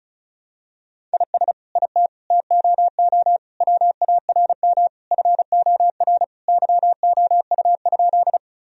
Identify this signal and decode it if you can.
Morse code, heard as a keyed tone, about 36 wpm, 700 Hz. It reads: IS IT TOO WARM FOR YOU?
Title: Sentences from Top 1000 Words - Review Part 2 - 35wpm